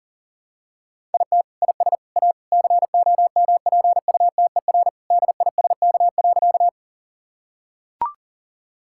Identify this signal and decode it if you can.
Morse code, heard as a keyed tone, about 40 wpm, 700 Hz. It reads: IT IS A COMPUTER DISK.
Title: Sentences from Top 1000 Words - Review Part 6 - 40wpm